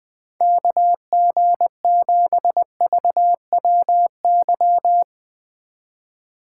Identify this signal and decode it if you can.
Morse code, heard as a keyed tone, about 20 wpm, 700 Hz. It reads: KG7VWY
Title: Speed Racing - USA 2x3 Call Signs 17wpm